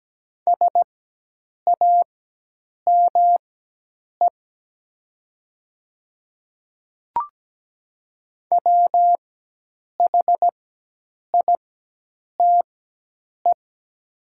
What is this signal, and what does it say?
Morse code, heard as a keyed tone, about 17 wpm, spaced as for 8 wpm, 700 Hz. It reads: SAME WHITE
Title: 4x Character Spacing - Top 300 Words - 17wpm